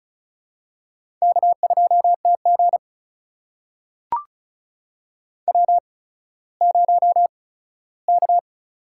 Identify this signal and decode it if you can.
Morse code, heard as a keyed tone, about 36 wpm, spaced as for 17 wpm, 700 Hz. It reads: K2TG W0K
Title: USA Call Signs - 8x Character Spacing - 35wpm